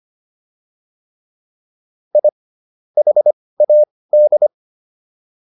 Morse code, heard as a keyed tone, about 25 wpm, spaced as for 18 wpm, 600 Hz. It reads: I HAD